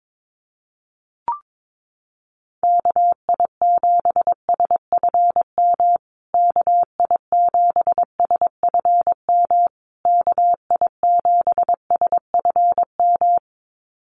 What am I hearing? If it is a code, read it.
Morse code, heard as a keyed tone, about 22 wpm, 700 Hz. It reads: KI7SFM KI7SFM KI7SFM
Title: USA Call Signs Repeated 3x 22wpm